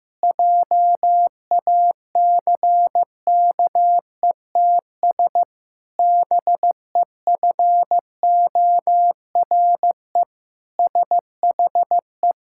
Morse code, heard as a keyed tone, about 16 wpm, 700 Hz. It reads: JACKETS BEFORE SHE
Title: Pangram Challenge v2 - 15wpm